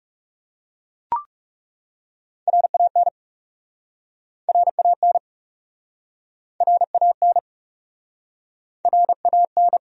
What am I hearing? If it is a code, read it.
Morse code, heard as a keyed tone, about 40 wpm, 700 Hz. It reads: RAN RAN RAN RAN